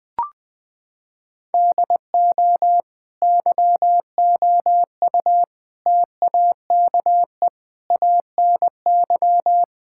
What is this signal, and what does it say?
Morse code, heard as a keyed tone, about 20 wpm, 700 Hz. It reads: DO YOU TAKE ANY
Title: Sentences from Top 1000 Words - Review Part 6 - 20wpm